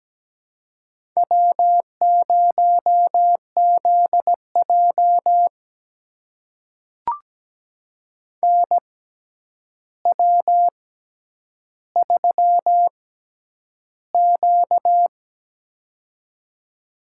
Morse code, heard as a keyed tone, about 17 wpm, spaced as for 9 wpm, 700 Hz. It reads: W0ZJ NW3Q
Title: USA Call Signs - 4 Characters - 6x Character Spacing - 17wpm